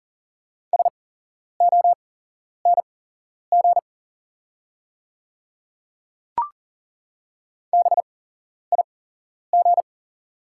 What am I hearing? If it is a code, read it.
Morse code, heard as a keyed tone, about 40 wpm, spaced as for 11 wpm, 700 Hz. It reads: SONG BIG
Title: 8x Character Spacing - Top 500 Words - 40wpm